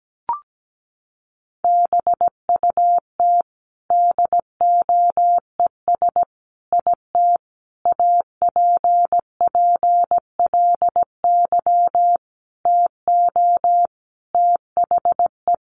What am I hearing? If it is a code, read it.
Morse code, heard as a keyed tone, about 17 wpm, 700 Hz. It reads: BUT DOES IT APPLY TO THE
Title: Sentences from Top 1200 Words 17wpm